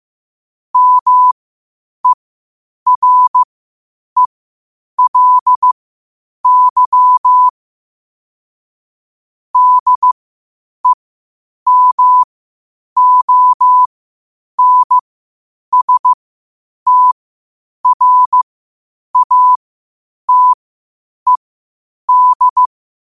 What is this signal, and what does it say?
Morse code, heard as a keyed tone, about 15 wpm, spaced as for 8 wpm, 1000 Hz. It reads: MERELY DEMONSTRATED
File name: qotdfiles / 8wpm